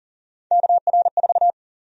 Morse code, heard as a keyed tone, about 40 wpm, 700 Hz. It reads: KR4
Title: USA Call Sign Prefixes - Repeated 2x - 40wpm